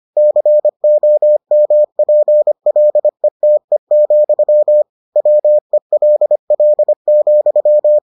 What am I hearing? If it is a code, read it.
Morse code, heard as a keyed tone, about 25 wpm, 600 Hz. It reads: COMPLETE, WELL,